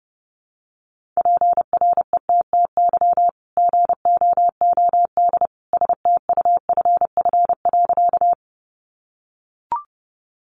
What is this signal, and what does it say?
Morse code, heard as a keyed tone, about 28 wpm, 700 Hz. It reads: PRETTY GOOD STUFF.
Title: Sentences from Top 1400 Words 30wpm